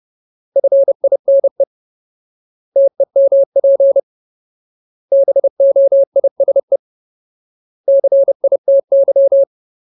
Morse code, heard as a keyed tone, about 30 wpm, 550 Hz. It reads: FINE TEMP BOISE CITY
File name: Ultimate Word List / 30WPM Ultimate